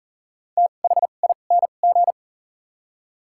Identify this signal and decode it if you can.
Morse code, heard as a keyed tone, about 40 wpm, spaced as for 29 wpm, 700 Hz. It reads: THING
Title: 2x Character Spacing - Top 200 Words - 40wpm